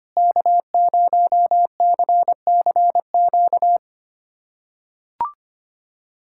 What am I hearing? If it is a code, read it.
Morse code, heard as a keyed tone, about 25 wpm, 700 Hz. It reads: K0CCQ
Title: USA Callsign DE Callsign - 5 Characters - 25wpm